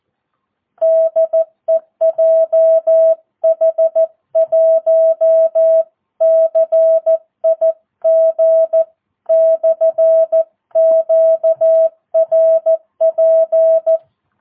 Morse code, heard as a keyed tone, about 14 wpm, 660 Hz. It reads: DE JH1CIG/QRP
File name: jh8cbh / 2022cw